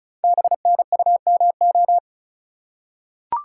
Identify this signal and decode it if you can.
Morse code, heard as a keyed tone, about 36 wpm, 690 Hz. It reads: BNUMO
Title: Code Groups - 5 Letters - 35wpm